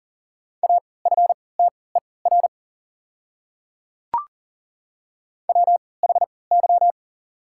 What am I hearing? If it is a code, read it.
Morse code, heard as a keyed tone, about 40 wpm, spaced as for 23 wpm, 700 Hz. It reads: AFTER WHY
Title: 3x Character Spacing - Top 200 Words - 40wpm